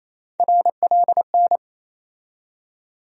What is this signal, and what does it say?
Morse code, heard as a keyed tone, about 28 wpm, 700 Hz. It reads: RLN
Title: USA Call Sign Postfixes - 3 Letters 28wpm